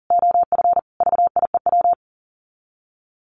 Morse code, heard as a keyed tone, about 40 wpm, 700 Hz. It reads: OF VIEW
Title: Sets of 5 Words 40wpm